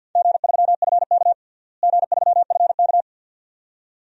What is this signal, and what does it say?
Morse code, heard as a keyed tone, about 51 wpm, 700 Hz. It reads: G3FK G3FK
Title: DXCC-100 Call Signs - 4 Characters - Repeated 2x - 50wpm